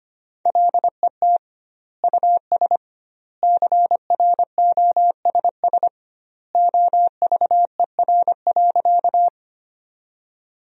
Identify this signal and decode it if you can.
Morse code, heard as a keyed tone, about 25 wpm, 700 Hz. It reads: LET US CROSS OVER.
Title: 2x Word-Spacing - Sentences from Top 300 Words - 25wpm